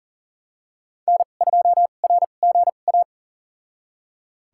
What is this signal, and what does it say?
Morse code, heard as a keyed tone, about 40 wpm, spaced as for 29 wpm, 700 Hz. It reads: N2RGA
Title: USA Call Signs - 5 Characters - 2x Character Spacing - 40wpm